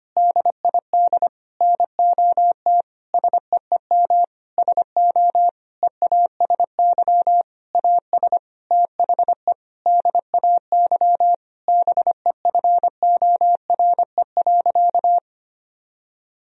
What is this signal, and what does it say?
Morse code, heard as a keyed tone, about 25 wpm, 700 Hz. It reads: DID NOT SEEM SO EASY AS THE DAY BEFORE.
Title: Sentences from Top 1000 Words - Review Part 1 - 25wpm